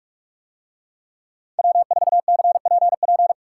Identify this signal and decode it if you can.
Morse code, heard as a keyed tone, about 44 wpm, 700 Hz. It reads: W4CPP